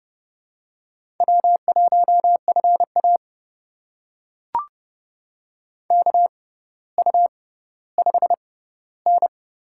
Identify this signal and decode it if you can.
Morse code, heard as a keyed tone, about 30 wpm, spaced as for 15 wpm, 700 Hz. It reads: W1FA KU5N
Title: USA Call Signs - 4 Characters - 6x Character Spacing - 30wpm